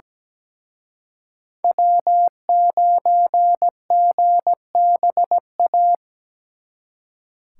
Morse code, heard as a keyed tone, about 20 wpm, 700 Hz. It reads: W9GBA
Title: Speed Racing - USA 1x3 Call Signs 17wpm